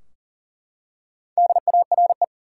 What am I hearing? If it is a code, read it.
Morse code, heard as a keyed tone, about 38 wpm, 700 Hz. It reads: DARE
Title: Four Letter Words - 40wpm